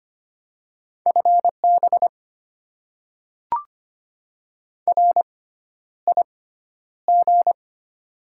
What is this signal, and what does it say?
Morse code, heard as a keyed tone, about 25 wpm, spaced as for 11 wpm, 690 Hz. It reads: FB RIG
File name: Top 20 QSO Elements - 6x Character Spacing - 25wpm